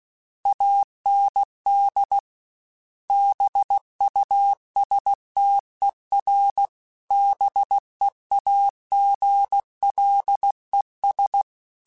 Morse code, heard as a keyed tone, about 16 wpm, 800 Hz. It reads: AND BUSTER BEAGLES
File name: Misc / Misc-mp3s / BIG @15